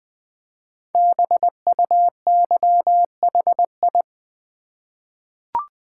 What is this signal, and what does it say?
Morse code, heard as a keyed tone, about 20 wpm, 700 Hz. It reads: BUYHI
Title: Code Groups - 5 Letters - 20wpm